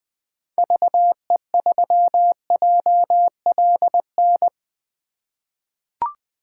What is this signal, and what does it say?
Morse code, heard as a keyed tone, about 20 wpm, 700 Hz. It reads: VE3JLN
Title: 2023 POTA Activator Call Signs - 20wpm